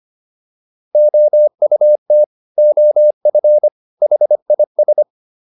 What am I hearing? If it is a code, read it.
Morse code, heard as a keyed tone, about 25 wpm, 600 Hz. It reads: OUT OF HIS